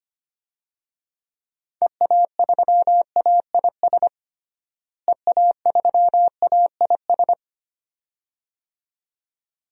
Morse code, heard as a keyed tone, about 25 wpm, 700 Hz. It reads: EA3AIS EA3AIS
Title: DXCC-100 Call Signs - 6 Characters - Repeated 2x - 25wpm